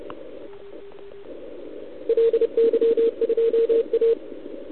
Morse code, heard as a keyed tone, about 30 wpm, 460 Hz. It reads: LY2A
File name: ARCHIV / OKDX2019 / OKOM40 KX3